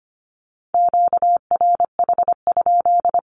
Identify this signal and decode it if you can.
Morse code, heard as a keyed tone, about 25 wpm, 700 Hz. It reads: QRH?
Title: Q-Codes 25wpm